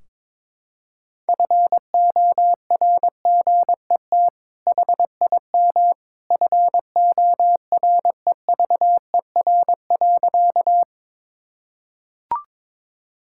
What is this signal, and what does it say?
Morse code, heard as a keyed tone, about 22 wpm, 700 Hz. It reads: FORGET HIM FOREVER.
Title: Sentences from Top 1600 Words 22wpm